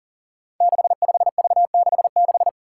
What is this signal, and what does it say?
Morse code, heard as a keyed tone, about 40 wpm, 700 Hz. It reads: 65V66
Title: Five Challenging Letter-Number Combinations 40wpm